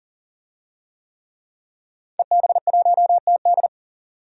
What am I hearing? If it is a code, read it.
Morse code, heard as a keyed tone, about 39 wpm, 700 Hz. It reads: EB1TD
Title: DXCC-100 Call Signs - 5 Characters - 40wpm